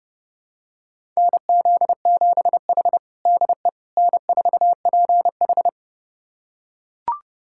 Morse code, heard as a keyed tone, about 30 wpm, 700 Hz. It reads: NZ7H DE N4PH
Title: USA Callsign DE Callsign - 4 Characters - 30wpm